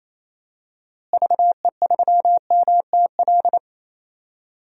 Morse code, heard as a keyed tone, about 28 wpm, 700 Hz. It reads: VE3MTL
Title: DXCC-100 Call Signs - 6 Characters - Repeated 2x - 28wpm